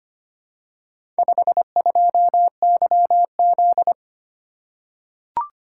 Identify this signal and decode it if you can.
Morse code, heard as a keyed tone, about 25 wpm, 700 Hz. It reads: 52YZ
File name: practice / 25wpm / Four Letter-Number Combinations - 25wpm